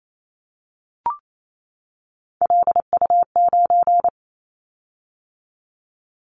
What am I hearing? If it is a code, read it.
Morse code, heard as a keyed tone, about 28 wpm, 700 Hz. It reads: LU9
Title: Callsign Prefixes for Top 50 DXCC - 28wpm